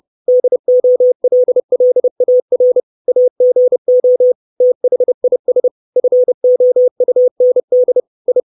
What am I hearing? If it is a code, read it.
Morse code, heard as a keyed tone, about 30 wpm, 500 Hz. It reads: DOLLAR AGO THIS FOUND I
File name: 632 Words 30WPM